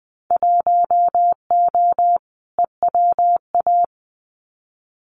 Morse code, heard as a keyed tone, about 19 wpm, 700 Hz. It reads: 1O EWA
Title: Winter Field Day Exchanges Encore 20wpm